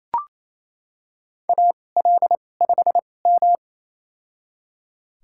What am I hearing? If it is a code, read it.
Morse code, heard as a keyed tone, about 28 wpm, spaced as for 20 wpm, 700 Hz. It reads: AL5M
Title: USA Call Signs - 4 Characters - 2x Character Spacing - 28wpm